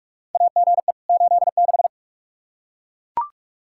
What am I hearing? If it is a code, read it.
Morse code, heard as a keyed tone, about 41 wpm, 700 Hz. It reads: AGE 86
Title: K1USN SST QSO Practice 45wpm